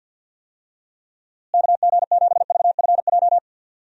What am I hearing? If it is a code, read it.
Morse code, heard as a keyed tone, about 51 wpm, 700 Hz. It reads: KG7VFJ
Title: USA Call Signs - 6 Characters - Repeated 3x - 50wpm